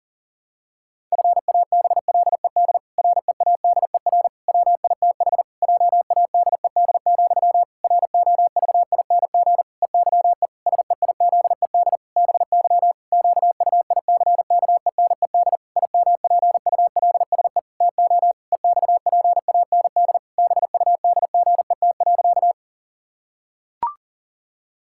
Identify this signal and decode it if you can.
Morse code, heard as a keyed tone, about 40 wpm, 700 Hz. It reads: FABLED READER WITH JADED, ROVING EYE SEIZED BY QUICKENED IMPULSE TO EXPAND BUDGET.
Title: Pangram Challenge v2 - 40wpm